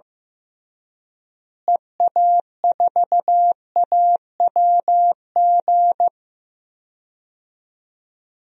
Morse code, heard as a keyed tone, about 15 wpm, 700 Hz. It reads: EA4AWG
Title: DXCC-100 Call Signs - 6 Characters - Repeated 3x - 15wpm